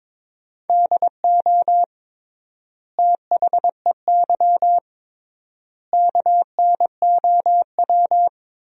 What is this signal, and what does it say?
Morse code, heard as a keyed tone, about 22 wpm, 700 Hz. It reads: DO THEY KNOW
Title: 3x Word-Spacing - Sentences from Top 100 Words - 22wpm